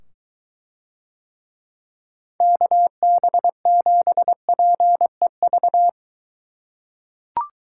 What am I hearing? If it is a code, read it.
Morse code, heard as a keyed tone, about 23 wpm, 700 Hz. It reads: KB7PEV